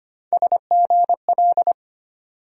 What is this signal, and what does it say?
Morse code, heard as a keyed tone, about 25 wpm, 700 Hz. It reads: SGL
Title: USA Call Sign Postfixes - 3 Letters 25wpm